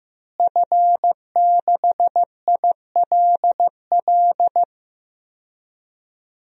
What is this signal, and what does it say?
Morse code, heard as a keyed tone, about 15 wpm, 700 Hz. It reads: F6ILL
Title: DXCC-100 Call Signs - 5 Characters - Repeated 2x - 15wpm